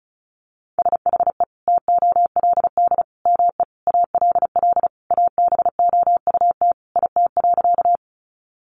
Morse code, heard as a keyed tone, about 34 wpm, 700 Hz. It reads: SHE TOLD ME ALL ABOUT IT.
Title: Sentences from Top 400 Words 35wpm